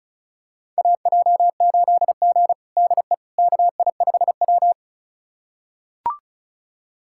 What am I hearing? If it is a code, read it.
Morse code, heard as a keyed tone, about 36 wpm, 700 Hz. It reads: AJ8G DE KI5W
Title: USA Callsign DE Callsign - 4 Characters - 35wpm